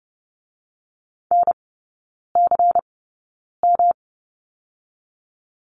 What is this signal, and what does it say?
Morse code, heard as a keyed tone, about 30 wpm, spaced as for 9 wpm, 700 Hz. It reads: NCM
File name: Lesson 18e - Letter m - Three Characters 30wpm